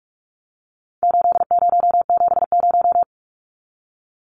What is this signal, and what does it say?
Morse code, heard as a keyed tone, about 44 wpm, 700 Hz. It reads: 8070
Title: Numbers - 4-Digit 45wpm